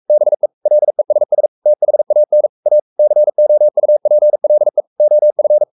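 Morse code, heard as a keyed tone, about 44 wpm, 600 Hz. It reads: BE LESS THAN A COUPLE OF
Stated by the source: Morse code at 43 wpm sending Ken Follett's novel Never in English, from CW med Gnister.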